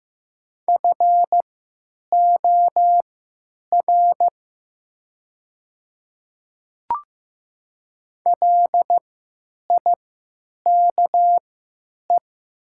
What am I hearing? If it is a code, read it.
Morse code, heard as a keyed tone, about 15 wpm, spaced as for 9 wpm, 700 Hz. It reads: FOR LIKE
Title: 3x Character Spacing - Top 100 Words - 15wpm